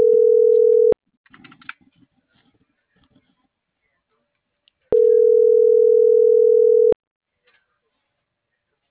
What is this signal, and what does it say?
Ringback tone (North America), 440 + 480 Hz.